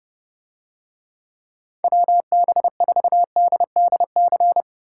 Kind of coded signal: Morse code

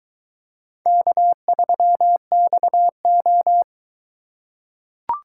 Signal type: Morse code

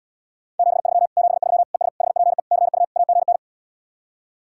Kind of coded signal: Morse code

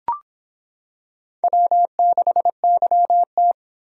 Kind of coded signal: Morse code

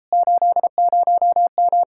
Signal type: Morse code

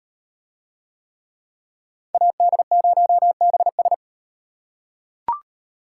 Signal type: Morse code

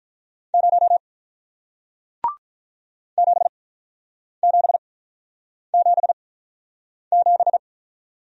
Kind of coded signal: Morse code

